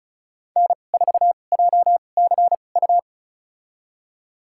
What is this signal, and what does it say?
Morse code, heard as a keyed tone, about 36 wpm, spaced as for 25 wpm, 700 Hz. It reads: N4JCU
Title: USA Call Signs - 5 Characters - 2x Character Spacing - 35wpm